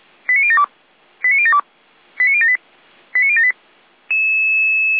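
Five-tone selective call (ZVEI), heard as quick runs of five tones, tones of about 70 ms. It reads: 89072 89072 89078 89078
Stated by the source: ZVEI five-tone call received by SDR.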